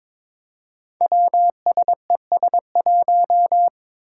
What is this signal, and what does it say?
Morse code, heard as a keyed tone, about 22 wpm, 700 Hz. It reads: WSES1
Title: Five Challenging Letter-Number Combinations 22wpm